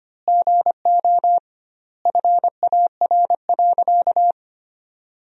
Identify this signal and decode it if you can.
Morse code, heard as a keyed tone, about 25 wpm, 700 Hz. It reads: GO FAR.
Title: 2x Word-Spacing - Sentences from Top 400 Words - 25wpm